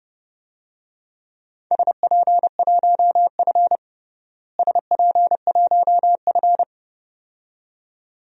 Morse code, heard as a keyed tone, about 30 wpm, 700 Hz. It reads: SP1F SP1F
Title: DXCC-100 Call Signs - 4 Characters - Repeated 2x - 30wpm